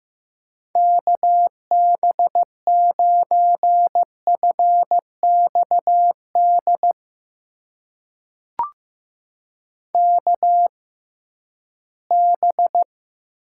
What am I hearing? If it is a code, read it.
Morse code, heard as a keyed tone, about 15 wpm, spaced as for 11 wpm, 700 Hz. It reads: KB9FXD KB
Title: USA Call Signs - 6 Characters - 6x Character Spacing - 15wpm